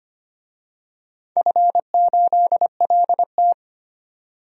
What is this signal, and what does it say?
Morse code, heard as a keyed tone, about 25 wpm, 700 Hz. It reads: F8LT
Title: DXCC-100 Call Signs - 4 Characters - 25wpm